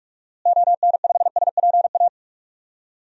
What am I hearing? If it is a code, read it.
Morse code, heard as a keyed tone, about 43 wpm, 700 Hz. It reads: ON5SPA
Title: DXCC-100 Call Signs - 6 Characters - Repeated 2x - 45wpm